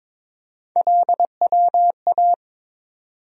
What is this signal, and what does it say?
Morse code, heard as a keyed tone, about 22 wpm, 700 Hz. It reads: LWA
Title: USA Call Sign Postfixes - 3 Letters 22wpm